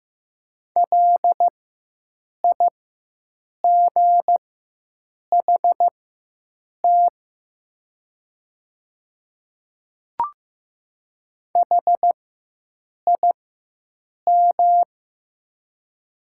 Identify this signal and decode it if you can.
Morse code, heard as a keyed tone, about 15 wpm, spaced as for 7 wpm, 700 Hz. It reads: LIGHT HIM
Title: 4x Character Spacing - Top 200 Words - 15wpm